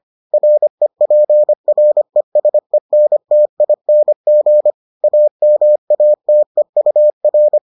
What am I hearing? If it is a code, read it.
Morse code, heard as a keyed tone, about 25 wpm, 600 Hz. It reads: REPRESENTING AMATEUR